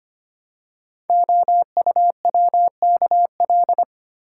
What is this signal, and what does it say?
Morse code, heard as a keyed tone, about 25 wpm, 700 Hz. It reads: OUWKL
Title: Code Groups - 5 Letters - 25wpm